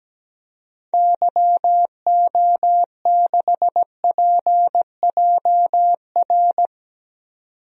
Morse code, heard as a keyed tone, about 17 wpm, 700 Hz. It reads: YO6PJR
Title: DXCC-100 Call Signs - 6 Characters - 17wpm